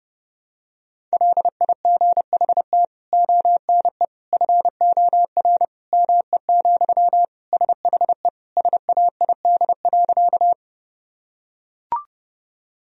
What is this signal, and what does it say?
Morse code, heard as a keyed tone, about 30 wpm, 700 Hz. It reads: LIGHT ONE FOR ME, SHE SAID.
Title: Sentences from Top 1000 Words - Review Part 1 - 30wpm